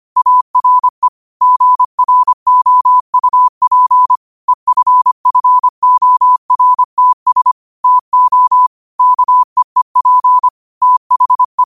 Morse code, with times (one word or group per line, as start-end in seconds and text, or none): 0.2-1.1 ARE
1.4-4.2 GROUP
4.5-7.5 EFFORTS
7.8-8.7 TO
9.0-10.5 KEEP
10.8-11.6 THE